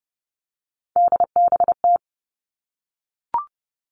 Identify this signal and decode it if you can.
Morse code, heard as a keyed tone, about 30 wpm, 700 Hz. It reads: DBT
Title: All Cut Numbers - 3-Digit 30wpm